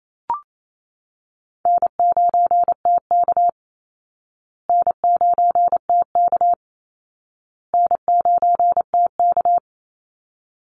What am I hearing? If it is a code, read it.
Morse code, heard as a keyed tone, about 28 wpm, 700 Hz. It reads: N9TK N9TK N9TK
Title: Easy Call Signs Repeated 3x - 28wpm